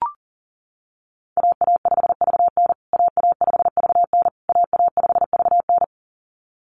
Morse code, heard as a keyed tone, about 40 wpm, 700 Hz. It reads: AA5VN AA5VN AA5VN
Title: USA Call Signs Repeated 3x 40wpm